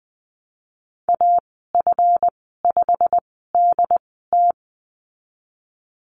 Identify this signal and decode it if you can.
Morse code, heard as a keyed tone, about 20 wpm, spaced as for 14 wpm, 700 Hz. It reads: AF5DT
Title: 140 Lesson 23g - Letter f - Call Signs 20wpm